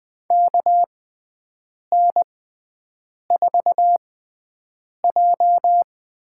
Morse code, heard as a keyed tone, about 21 wpm, spaced as for 7 wpm, 700 Hz. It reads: KN4J
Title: USA Call Signs - 6 Characters - 6x Character Spacing - 20wpm